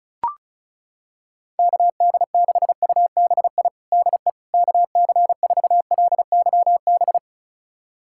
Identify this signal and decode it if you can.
Morse code, heard as a keyed tone, about 36 wpm, 700 Hz. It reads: KD6UBI DE KC4LYB
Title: USA Callsign DE Callsign - 6 Characters - 35wpm